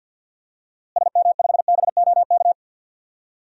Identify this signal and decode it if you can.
Morse code, heard as a keyed tone, about 52 wpm, 700 Hz. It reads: SM5BOK
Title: DXCC-100 Call Signs - 6 Characters - Repeated 3x - 50wpm